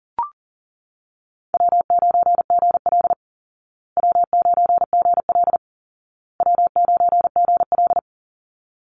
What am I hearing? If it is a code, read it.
Morse code, heard as a keyed tone, about 40 wpm, 700 Hz. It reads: W9GL W9GL W9GL
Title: Call Signs Repeated 2x - 40wpm